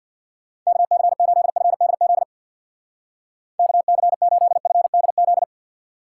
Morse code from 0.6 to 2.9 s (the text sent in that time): KC8FDZ